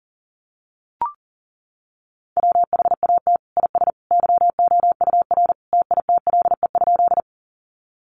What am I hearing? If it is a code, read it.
Morse code, heard as a keyed tone, about 37 wpm, 700 Hz. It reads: WHAT IS YOUR TITLE?
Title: Sentences from Top 1200 Words 40wpm